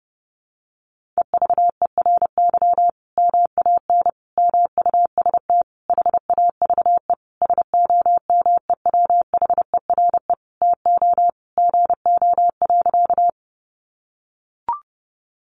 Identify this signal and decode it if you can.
Morse code, heard as a keyed tone, about 30 wpm, 700 Hz. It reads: EVERY MAN MUST HAVE SOMEWHERE TO GO.
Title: Sentences from Top 1400 Words 30wpm